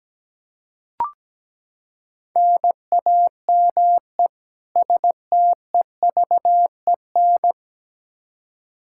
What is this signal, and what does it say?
Morse code, heard as a keyed tone, about 17 wpm, 700 Hz. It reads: NAME STEVEN
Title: Practice QSO - Exchange 1 - Beginning - 17wpm